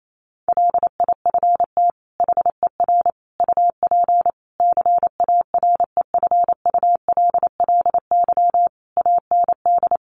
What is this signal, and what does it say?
Morse code, heard as a keyed tone, about 28 wpm, 700 Hz. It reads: LIFT HER UP CAREFULLY AND
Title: Sentences from Top 1300 Words 28wpm